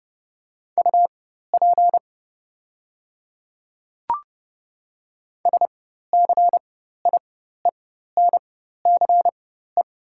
Morse code, heard as a keyed tone, about 30 wpm, spaced as for 14 wpm, 700 Hz. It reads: UP SCIENCE
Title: 4x Character Spacing - Top 500 Words - 30wpm